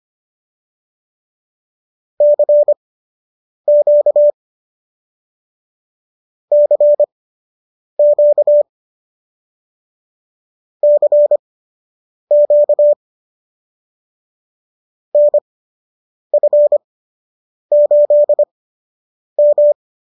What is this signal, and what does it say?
Morse code, heard as a keyed tone, about 25 wpm, spaced as for 8 wpm, 590 Hz. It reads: CQ CQ CQ NF8M